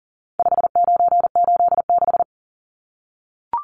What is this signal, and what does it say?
Morse code, heard as a keyed tone, about 40 wpm, 700 Hz. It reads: 5986
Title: Numbers - 4-Digit 40wpm